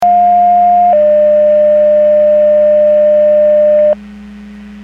Two-tone page, heard as a one-second tone, then a three-second tone, about 710 then 600 Hz. It is heard from 0.0 to 3.9 s.